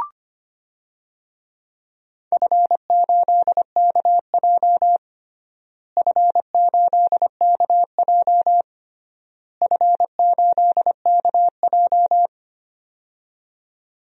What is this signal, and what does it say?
Morse code, heard as a keyed tone, about 25 wpm, 700 Hz. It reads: F8KJ F8KJ F8KJ
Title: DXCC-100 Call Signs - 4 Characters - Repeated 3x - 25wpm